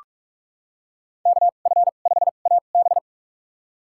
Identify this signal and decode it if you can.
Morse code, heard as a keyed tone, about 47 wpm, spaced as for 33 wpm, 690 Hz. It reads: KF5AB
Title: USA Call Signs - 5 Characters - 2x Character Spacing - 45wpm